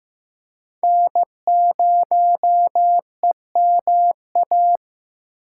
Morse code, heard as a keyed tone, about 15 wpm, 700 Hz. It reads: N0EMA
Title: USA Call Signs - 5 Characters - 6x Character Spacing - 15wpm